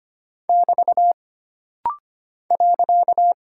Morse code, heard as a keyed tone, about 25 wpm, 700 Hz. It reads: <BT> .
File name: Single Punctuation-Prosign - Rapid-Fire - 25wpm